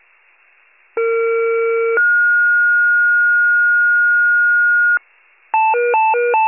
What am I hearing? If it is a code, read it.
Two-tone page, heard as a one-second tone, then a three-second tone, about 460 then 1430 Hz. It is heard from 1.0 to 5.0 s.
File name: pager / 21 FIRE / wcalert